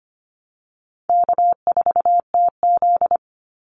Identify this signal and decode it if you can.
Morse code, heard as a keyed tone, about 25 wpm, 700 Hz. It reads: K4TZ
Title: Easy Call Signs Repeated 2x - 25wpm